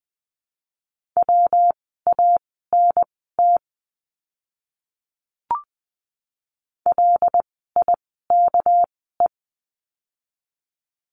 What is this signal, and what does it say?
Morse code, heard as a keyed tone, about 20 wpm, spaced as for 15 wpm, 700 Hz. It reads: WANT LIKE
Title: 206 Lesson 33f - Letter j - Words 20wpm